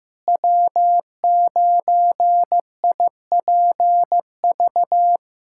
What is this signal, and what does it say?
Morse code, heard as a keyed tone, about 15 wpm, 700 Hz. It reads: W9IPV
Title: USA Callsign DE Callsign - 5 Characters - 15wpm